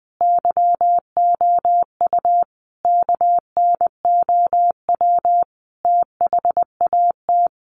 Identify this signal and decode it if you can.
Morse code, heard as a keyed tone, about 20 wpm, 700 Hz. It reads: YOU KNOW THAT
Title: Sets of 5 Words Encore 20wpm